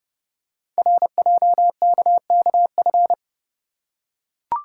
Morse code, heard as a keyed tone, about 30 wpm, 690 Hz. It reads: RJKKF
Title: Code Groups - 5 Letters - 30wpm